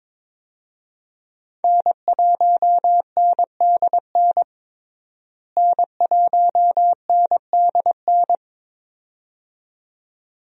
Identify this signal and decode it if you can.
Morse code, heard as a keyed tone, about 22 wpm, 700 Hz. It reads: N1NDN N1NDN
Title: USA Call Signs - 5 Characters - Repeated 2x - 22wpm